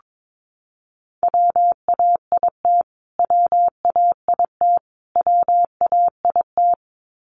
Morse code, heard as a keyed tone, about 22 wpm, 700 Hz. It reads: WAIT WAIT WAIT
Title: Top 500 Words Repeated 22wpm